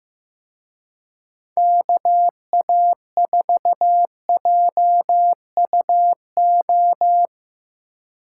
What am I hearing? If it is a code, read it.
Morse code, heard as a keyed tone, about 16 wpm, 700 Hz. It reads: KA4JUO
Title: Speed Racing - USA 2x3 Call Signs 15wpm